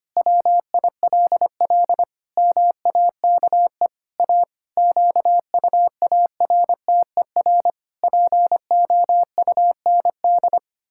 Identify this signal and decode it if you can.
Morse code, heard as a keyed tone, about 25 wpm, 700 Hz. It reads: WILL MAKE A QUARTER POUND
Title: Pangram Challenge v2 - 25wpm